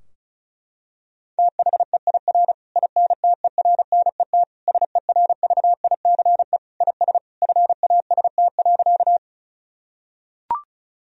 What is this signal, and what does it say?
Morse code, heard as a keyed tone, about 36 wpm, 700 Hz. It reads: THEIR INTERNET SERVICE IS FAST.
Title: Sentences from Top 1000 Words - Review Part 4 - 35wpm